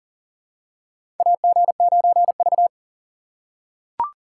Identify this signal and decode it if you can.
Morse code, heard as a keyed tone, about 37 wpm, 700 Hz. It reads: AG9V
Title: USA Call Signs - 4 Characters - 4x Character Spacing - 40wpm